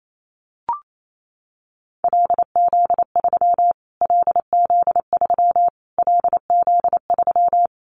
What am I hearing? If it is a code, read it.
Morse code, heard as a keyed tone, about 28 wpm, 700 Hz. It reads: LZ3 LZ3 LZ3
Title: Callsign Prefixes for Top 25 DXCC - Repeated 3x - 28wpm